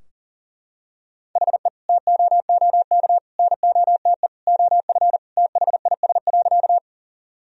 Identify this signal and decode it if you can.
Morse code, heard as a keyed tone, about 40 wpm, 700 Hz. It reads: HE TOOK NOTE OF THIS.
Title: Sentences from Top 1000 Words - Review Part 2 - 40wpm